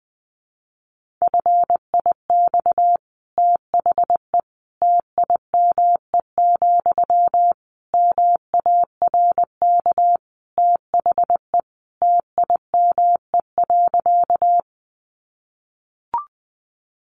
Morse code, heard as a keyed tone, about 20 wpm, 700 Hz. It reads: FIX THE TIME, MARK THE TIME.
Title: Sentences from Top 1400 Words 20wpm